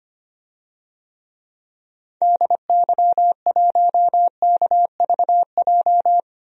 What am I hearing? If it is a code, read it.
Morse code, heard as a keyed tone, about 25 wpm, 700 Hz. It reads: DY1KVJ
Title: DXCC-100 Call Signs - 6 Characters - 25wpm